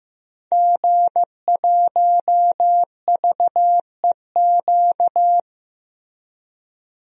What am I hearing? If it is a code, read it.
Morse code, heard as a keyed tone, about 15 wpm, 700 Hz. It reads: G1VEQ
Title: DXCC-100 Call Signs - 5 Characters - 15wpm